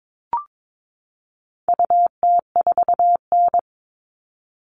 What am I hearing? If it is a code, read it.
Morse code, heard as a keyed tone, about 22 wpm, 700 Hz. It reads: UT4N
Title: All Cut Numbers - 4-Digit 22wpm